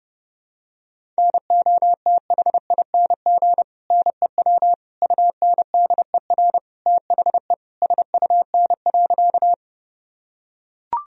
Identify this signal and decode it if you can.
Morse code, heard as a keyed tone, about 30 wpm, 700 Hz. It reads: NOTHING NEW UNDER THE SUN.
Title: Sentences from Top 1000 Words - Review Part 1 - 30wpm